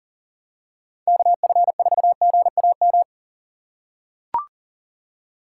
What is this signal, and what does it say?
Morse code, heard as a keyed tone, about 40 wpm, 700 Hz. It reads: KF4GAM
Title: USA Call Signs - 4x Character Spacing - 40wpm